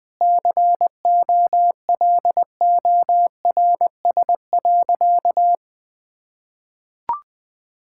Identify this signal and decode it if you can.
Morse code, heard as a keyed tone, about 21 wpm, 700 Hz. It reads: COLORS.
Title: Difficult Sentences with Hints - 20wpm